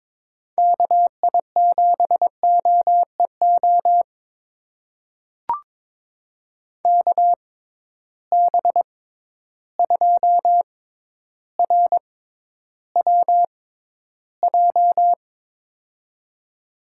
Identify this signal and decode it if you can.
Morse code, heard as a keyed tone, about 22 wpm, spaced as for 11 wpm, 700 Hz. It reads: KI7OEO KB2RWJ
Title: USA Call Signs - 6 Characters - 6x Character Spacing - 22wpm